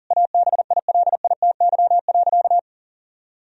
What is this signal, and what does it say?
Morse code, heard as a keyed tone, about 40 wpm, 700 Hz. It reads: ABILITY.
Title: Sentences from Top 1000 Words - Review Part 3 - 40wpm